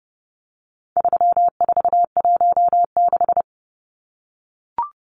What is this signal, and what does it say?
Morse code, heard as a keyed tone, about 30 wpm, 700 Hz. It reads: 3416
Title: Numbers - 4-Digit 30wpm